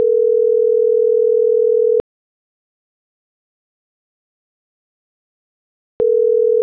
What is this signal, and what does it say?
Ringback tone (North America), 440 + 480 Hz.
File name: CENTRAL MAERSK / out yunuel.morales